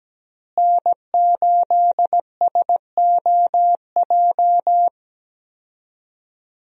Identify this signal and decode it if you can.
Morse code, heard as a keyed tone, about 17 wpm, 700 Hz. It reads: N8SOJ
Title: USA Call Signs - 5 Characters - Repeated 2x - 17wpm